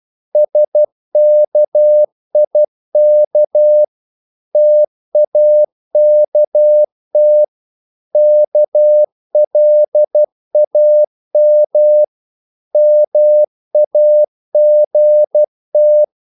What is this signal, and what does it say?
Morse code, heard as a keyed tone, about 12 wpm, 600 Hz. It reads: SKIK TAKT KLAM MAGT
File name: Korte ord DK 12wpm